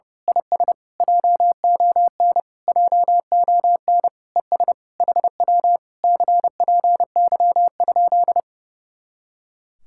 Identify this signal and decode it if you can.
Morse code, heard as a keyed tone, about 28 wpm, 700 Hz. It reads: IS JON JON ES HW CPY?
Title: Speed-Racing Common QSO Phrases - 30wpm